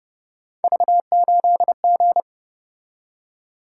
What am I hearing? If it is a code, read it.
Morse code, heard as a keyed tone, about 30 wpm, 700 Hz. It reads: V8G
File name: Speed-Racing - Three Letter-Number Combinations - 25wpm